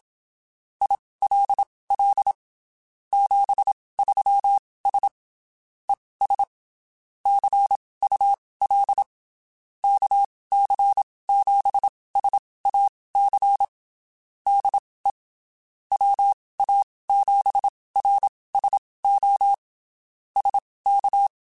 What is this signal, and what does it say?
Morse code, heard as a keyed tone, about 27 wpm, spaced as for 19 wpm, 800 Hz. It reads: ILL 73S ES CUL KC7SAC DE WA7RSO SK
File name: (spaced) / QSOs Spaced-mp3s / kc7sac @25 sp